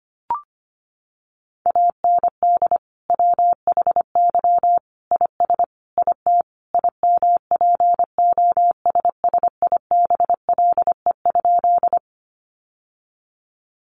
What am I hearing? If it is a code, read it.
Morse code, heard as a keyed tone, about 25 wpm, 700 Hz. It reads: AND WHY IS IT IMPOSSIBLE?
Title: Sentences from Top 1100 Words 25wpm